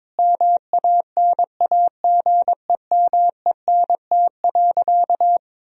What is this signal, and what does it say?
Morse code, heard as a keyed tone, about 22 wpm, 700 Hz. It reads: MANAGEMENT.
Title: Sentences from Top 1000 Words - Review Part 4 - 22wpm